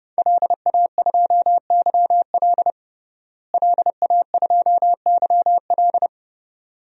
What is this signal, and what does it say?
Morse code, heard as a keyed tone, about 29 wpm, 700 Hz. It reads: LA2YL LA2YL
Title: DXCC-100 Call Signs - 5 Characters - Repeated 2x - 30wpm